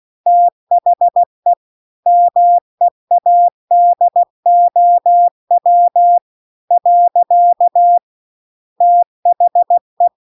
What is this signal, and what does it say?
Morse code, heard as a keyed tone, about 16 wpm, 700 Hz. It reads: THE MEADOW . THE